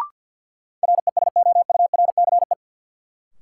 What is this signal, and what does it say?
Morse code, heard as a keyed tone, about 53 wpm, 700 Hz. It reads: RESOURCE
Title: Sentences from Top 1300 Words - New Word Drill - 50wpm